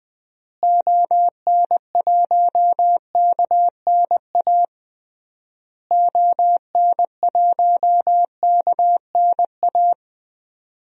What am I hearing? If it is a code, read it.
Morse code, heard as a keyed tone, about 20 wpm, 700 Hz. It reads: ON1KNA ON1KNA
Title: DXCC-100 Call Signs - 6 Characters - Repeated 3x - 20wpm